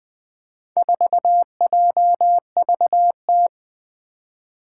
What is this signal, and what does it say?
Morse code, heard as a keyed tone, about 20 wpm, 700 Hz. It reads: 4JVT
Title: Four Challenging Letter-Number Combinations 20wpm